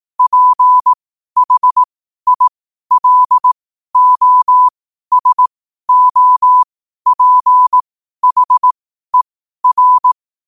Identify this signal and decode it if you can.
Morse code, heard as a keyed tone, about 18 wpm, spaced as for 12 wpm, 1000 Hz. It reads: PHILOSOPHER
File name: qotdfiles / 12wpm